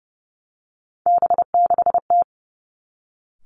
Morse code, heard as a keyed tone, about 30 wpm, 700 Hz. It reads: B6T
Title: Three Challenging Letter-Number Combinations 30wpm